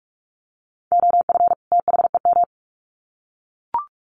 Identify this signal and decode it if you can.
Morse code, heard as a keyed tone, about 44 wpm, 700 Hz. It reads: OF THEM
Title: Sets of 2 Words 45wpm